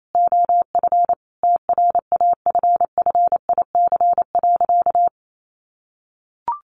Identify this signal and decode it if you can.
Morse code, heard as a keyed tone, about 28 wpm, 700 Hz. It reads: OF TRAFFIC.
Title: Sentences from Top 1300 Words 28wpm